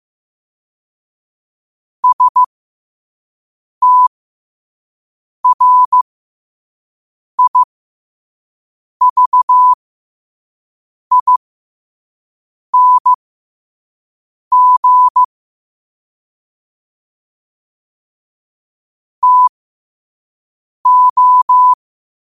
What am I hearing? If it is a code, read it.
Morse code, heard as a keyed tone, about 15 wpm, spaced as for 5 wpm, 1000 Hz. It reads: STRIVING TO
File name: qotdfiles / 5wpm